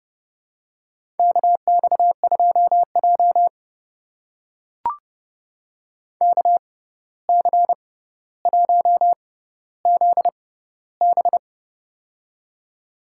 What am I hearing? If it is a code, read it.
Morse code, heard as a keyed tone, about 32 wpm, spaced as for 15 wpm, 700 Hz. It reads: KX2J KC1ZB
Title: USA Call Signs - 6x Character Spacing - 30wpm